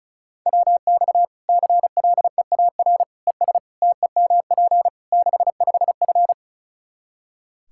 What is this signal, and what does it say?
Morse code, heard as a keyed tone, about 34 wpm, 700 Hz. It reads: WX CLEAR ES TEMP 65F
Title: Practice QSO - Exchange 2 - Info - 35wpm